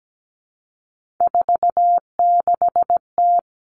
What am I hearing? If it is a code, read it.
Morse code, heard as a keyed tone, about 17 wpm, 700 Hz. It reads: 46T
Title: Three Challenging Letter-Number Combinations 17wpm